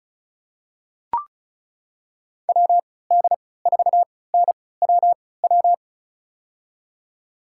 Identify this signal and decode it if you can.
Morse code, heard as a keyed tone, about 36 wpm, spaced as for 20 wpm, 700 Hz. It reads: WD4NWW
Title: USA Call Signs - 6 Characters - 3x Character Spacing - 35wpm